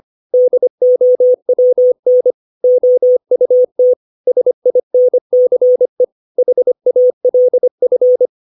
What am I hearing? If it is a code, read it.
Morse code, heard as a keyed tone, about 25 wpm, 500 Hz. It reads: DOWN OUT SINCE HALF
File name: 632 words / 632 Words 25WPM